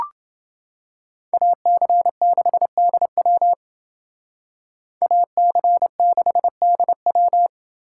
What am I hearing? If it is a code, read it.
Morse code, heard as a keyed tone, about 28 wpm, 700 Hz. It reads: AC6DW AC6DW